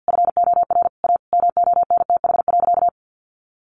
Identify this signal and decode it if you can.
Morse code, heard as a keyed tone, about 50 wpm, 700 Hz. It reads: FOR A MONTH.
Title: Easy Phrases 50wpm